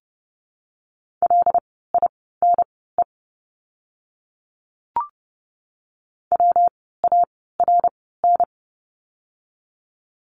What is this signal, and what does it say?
Morse code, heard as a keyed tone, about 30 wpm, spaced as for 17 wpm, 700 Hz. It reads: LINE WARN
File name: course / 30:10wpm / 115 Lesson 19f - Letter w - Words 30wpm